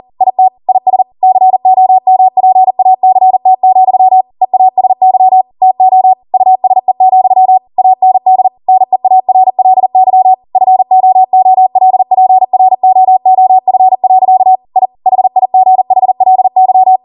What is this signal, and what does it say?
Morse code, heard as a keyed tone, about 40 wpm, 750 Hz. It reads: IT IS COMPACT, EASY TO USE, AND NEARLY FOOLPROOF. I HIGHLY